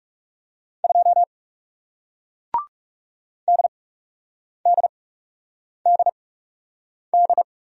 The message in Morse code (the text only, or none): 2 DDDD